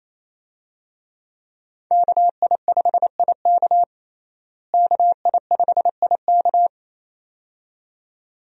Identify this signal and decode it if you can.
Morse code, heard as a keyed tone, about 28 wpm, 700 Hz. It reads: KI5IK KI5IK
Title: USA Call Signs - 5 Characters - Repeated 2x - 28wpm